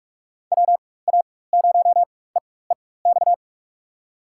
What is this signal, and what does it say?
Morse code, heard as a keyed tone, about 41 wpm, spaced as for 20 wpm, 700 Hz. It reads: WA0EEX
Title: USA Call Signs - 6 Characters - 4x Character Spacing - 45wpm